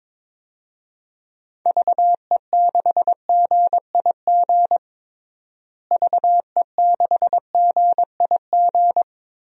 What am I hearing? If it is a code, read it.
Morse code, heard as a keyed tone, about 21 wpm, 700 Hz. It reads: VE6GIG VE6GIG
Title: DXCC-100 Call Signs - 6 Characters - Repeated 3x - 22wpm